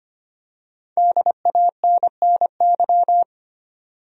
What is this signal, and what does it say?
Morse code, heard as a keyed tone, about 25 wpm, 700 Hz. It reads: DANNY
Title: USA Common Names - 25wpm